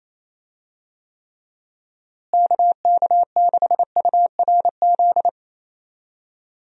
Morse code, heard as a keyed tone, about 28 wpm, 700 Hz. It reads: KK6URZ
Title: USA Call Signs - 6 Characters - Repeated 3x - 28wpm